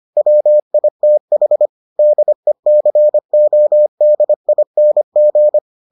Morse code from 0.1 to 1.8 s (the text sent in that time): WITH